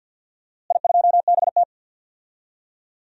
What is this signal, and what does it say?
Morse code, heard as a keyed tone, about 51 wpm, 700 Hz. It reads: I2BT
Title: DXCC-100 Call Signs - 4 Characters - Repeated 3x - 50wpm